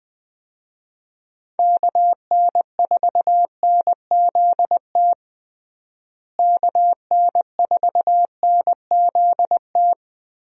Morse code, heard as a keyed tone, about 19 wpm, 700 Hz. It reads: KN4NZT KN4NZT
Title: USA Call Signs - 6 Characters - Repeated 3x - 20wpm